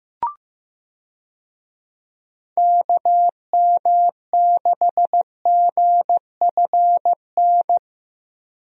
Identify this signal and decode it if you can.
Morse code, heard as a keyed tone, about 15 wpm, 700 Hz. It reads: KM6GFN
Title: USA Call Signs - 6 Characters - 15wpm